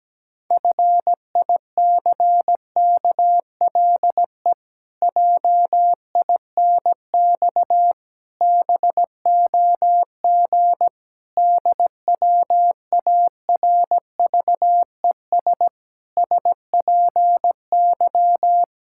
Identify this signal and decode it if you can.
Morse code, heard as a keyed tone, about 17 wpm, 700 Hz. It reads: FICKLE JINX BOG DWARVES SPY
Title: Pangram Challenge v2 - 17wpm